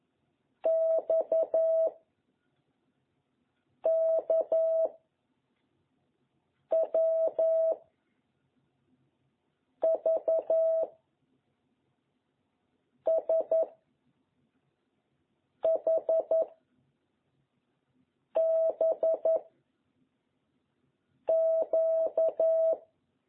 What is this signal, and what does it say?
Morse code, heard as a keyed tone, about 11 wpm, 660 Hz. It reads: X K W V S H B Q
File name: jh8cbh / 2021cw